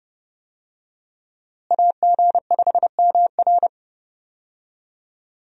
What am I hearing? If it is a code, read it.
Morse code, heard as a keyed tone, about 30 wpm, 700 Hz. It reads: AG5MR
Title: USA Call Signs - 5 Characters - Repeated 3x - 30wpm